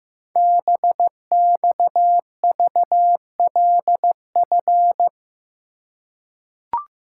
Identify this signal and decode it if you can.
Morse code, heard as a keyed tone, about 15 wpm, 700 Hz. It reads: BXVLF
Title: Code Groups - 5 Letters - 15wpm